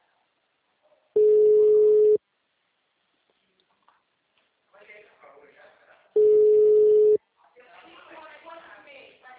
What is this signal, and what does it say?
Ringback tone (Europe, 425 Hz), with speech.